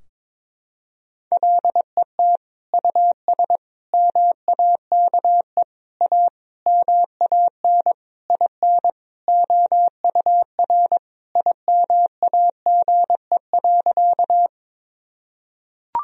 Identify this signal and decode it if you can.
Morse code, heard as a keyed tone, about 22 wpm, 700 Hz. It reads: LET US MAKE A MAN IN OUR IMAGE.
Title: Sentences from Top 1000 Words - Review Part 4 - 22wpm